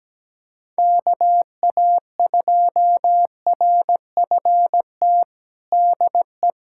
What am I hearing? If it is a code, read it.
Morse code, heard as a keyed tone, about 17 wpm, 700 Hz. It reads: KA2RFT DE
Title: USA Callsign DE Callsign 17wpm